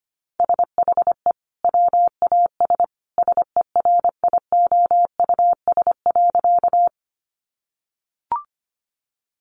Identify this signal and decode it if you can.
Morse code, heard as a keyed tone, about 25 wpm, 690 Hz. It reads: SHE WAS SERIOUS.
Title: Sentences from Top 1100 Words 25wpm